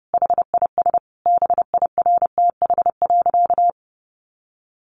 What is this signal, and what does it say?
Morse code, heard as a keyed tone, about 30 wpm, 700 Hz. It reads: HIS BIRTH.
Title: Sentences from Top 900 Words 30wpm